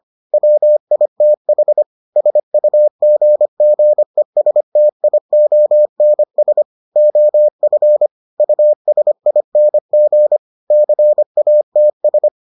Morse code, heard as a keyed tone, about 25 wpm, 600 Hz. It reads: WITH SUGGESTIONS OF USING CATS